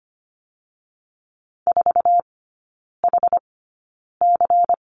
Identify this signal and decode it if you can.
Morse code, heard as a keyed tone, about 25 wpm, spaced as for 9 wpm, 700 Hz. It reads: 4HC
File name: course / 25:10wpm / 093 Lesson 16e - Letter c - Three Characters 25wpm